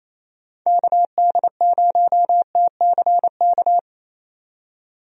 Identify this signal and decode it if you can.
Morse code, heard as a keyed tone, about 28 wpm, 700 Hz. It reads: KD0TCK
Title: USA Call Signs - 6 Characters - 28wpm